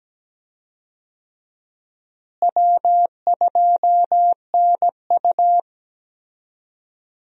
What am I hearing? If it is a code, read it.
Morse code, heard as a keyed tone, about 17 wpm, 700 Hz. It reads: W2NU
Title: USA Call Signs - 4 Characters - Repeated 3x - 17wpm